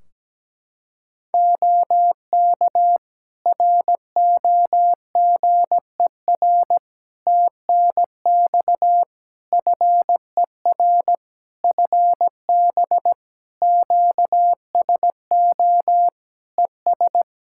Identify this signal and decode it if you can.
Morse code, heard as a keyed tone, about 17 wpm, 700 Hz. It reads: OK ROGER TNX FER FB QSO ES
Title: Practice QSO - Exchange 4 - Ending - 17wpm